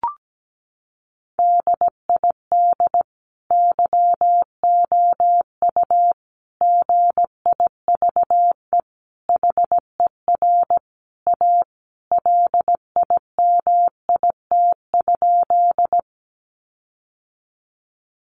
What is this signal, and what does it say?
Morse code, heard as a keyed tone, about 17 wpm, 700 Hz. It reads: DID YOU GIVE HER A LIMIT?
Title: Sentences from Top 1500 Words 17wpm